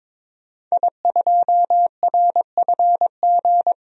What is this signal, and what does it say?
Morse code, heard as a keyed tone, about 21 wpm, 700 Hz. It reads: I2RFG
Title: DXCC-100 Call Signs - 5 Characters - 22wpm